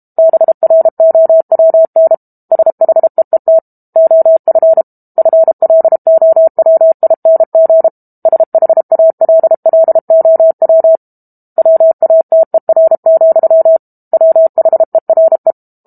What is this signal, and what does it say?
Morse code, heard as a keyed tone, about 32 wpm, 650 Hz. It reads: BROWN SHEET OF FLOWING SHALLOW WATER, WHERE
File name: Book Two / 06-Chapter 6 - 32 WPM